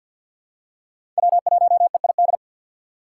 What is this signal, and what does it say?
Morse code, heard as a keyed tone, about 48 wpm, 700 Hz. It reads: W1EID
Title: USA Call Signs - 2x Character Spacing - 50wpm